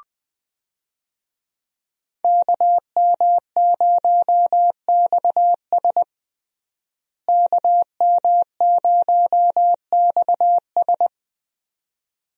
Morse code, heard as a keyed tone, about 20 wpm, 700 Hz. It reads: KM0XS KM0XS
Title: USA Call Signs - 5 Characters - Repeated 2x - 20wpm